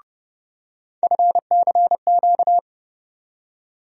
Morse code, heard as a keyed tone, about 30 wpm, 700 Hz. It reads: FCQ
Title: USA Call Sign Postfixes - 3 Letters 30wpm